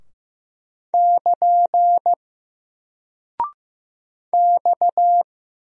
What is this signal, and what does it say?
Morse code, heard as a keyed tone, about 15 wpm, 700 Hz. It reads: <KN> X
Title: Single Character (Letter:Number:Punctuation:Prosign) Alt 15wpm